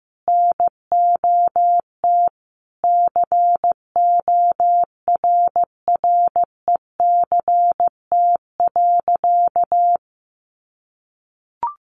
Morse code, heard as a keyed tone, about 15 wpm, 700 Hz. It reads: NOT CORRECT.
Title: Sentences from Top 1200 Words 15wpm